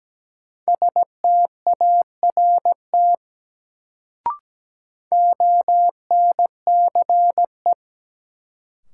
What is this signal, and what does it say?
Morse code, heard as a keyed tone, about 17 wpm, 700 Hz. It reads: START ONCE
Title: Sentences from Top 300 Words - New Word Drill - 17wpm